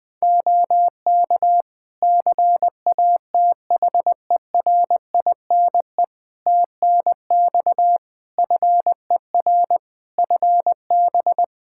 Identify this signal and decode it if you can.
Morse code, heard as a keyed tone, about 20 wpm, 700 Hz. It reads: OK CATHERINE TNX FER FB
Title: Practice QSO - Exchange 4 - Ending - 20wpm